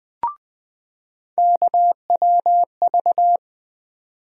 Morse code, heard as a keyed tone, about 21 wpm, 700 Hz. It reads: KWV
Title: USA Call Sign Postfixes - 3 Letters 20wpm